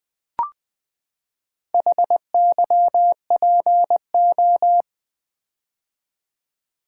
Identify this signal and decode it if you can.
Morse code, heard as a keyed tone, about 20 wpm, 700 Hz. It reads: HYPO
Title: Build A Word - 4 Letters At A Time - 10-Letter Words - 20wpm